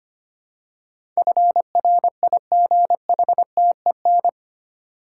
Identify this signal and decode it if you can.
Morse code, heard as a keyed tone, about 25 wpm, 700 Hz. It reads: FRIGHTEN
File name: Eight Letter Words - 25wpm